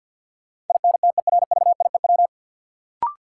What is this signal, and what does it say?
Morse code, heard as a keyed tone, about 50 wpm, 700 Hz. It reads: INTERVIEW
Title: Sentences from Top 1600 Words - New Word Drill Alt - 50wpm